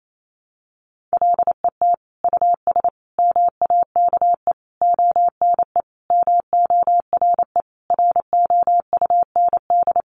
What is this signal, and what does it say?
Morse code, heard as a keyed tone, about 28 wpm, 700 Hz. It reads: LET US MAKE ONE MORE ROUND
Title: Sentences from Top 700 Words 28wpm